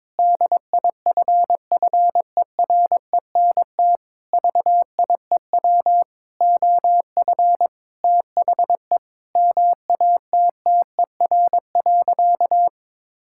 Morse code, heard as a keyed tone, about 22 wpm, 700 Hz. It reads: DIFFERENT VIEW OF THE MATTER.
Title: Sentences from Top 1600 Words 22wpm